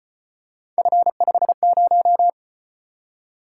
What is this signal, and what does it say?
Morse code, heard as a keyed tone, about 33 wpm, 700 Hz. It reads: F50